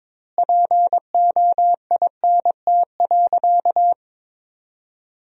Morse code, heard as a keyed tone, about 22 wpm, 700 Hz. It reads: POINT.
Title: 2x Word-Spacing - Sentences from Top 200 Words - 22wpm